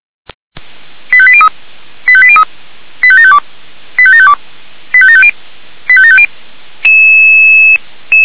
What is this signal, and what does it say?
Five-tone selective call (ZVEI), heard as quick runs of five tones, tones of about 70 ms. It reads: 85803 85803 85752 85752 85750 85750